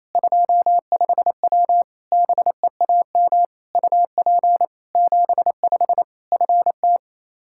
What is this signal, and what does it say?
Morse code, heard as a keyed tone, about 28 wpm, 700 Hz. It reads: 25W BEAM UP 75 FT
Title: Practice QSO - Exchange 2 - Info - 28wpm